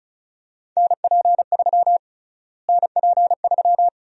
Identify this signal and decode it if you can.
Morse code, heard as a keyed tone, about 34 wpm, 700 Hz. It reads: NP3 NP3
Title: USA Call Sign Prefixes - Repeated 2x - 35wpm